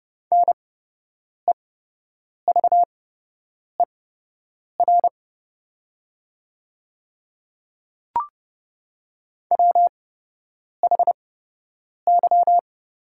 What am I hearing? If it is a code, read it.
Morse code, heard as a keyed tone, about 30 wpm, spaced as for 8 wpm, 700 Hz. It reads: NEVER WHY
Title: 8x Character Spacing - Top 300 Words - 30wpm